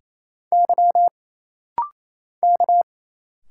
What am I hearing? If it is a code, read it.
Morse code, heard as a keyed tone, about 27 wpm, 700 Hz. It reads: Y K
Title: Single Letters - Rapid-Fire 28wpm